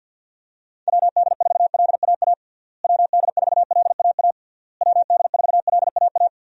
Morse code, heard as a keyed tone, about 52 wpm, 700 Hz. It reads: WD4LAA WD4LAA WD4LAA
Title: USA Call Signs - 6 Characters - Repeated 3x - 50wpm